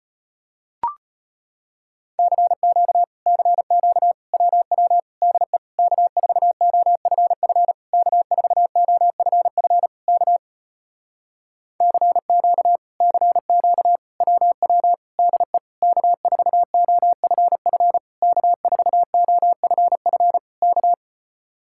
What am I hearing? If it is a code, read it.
Morse code, heard as a keyed tone, about 33 wpm, 700 Hz. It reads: CQ CQ WW DE K4OFF K4OFF K CQ CQ WW DE K4OFF K4OFF K